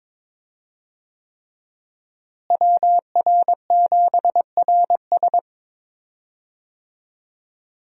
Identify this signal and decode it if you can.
Morse code, heard as a keyed tone, about 22 wpm, 700 Hz. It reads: WR7RS
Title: USA Call Signs - 5 Characters - Repeated 3x - 22wpm